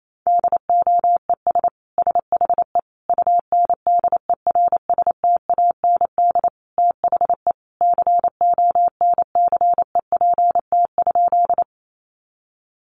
Morse code, heard as a keyed tone, about 28 wpm, 700 Hz. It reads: DOES SHE UNDERSTAND THE CONCEPT?
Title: Sentences from Top 1100 Words 28wpm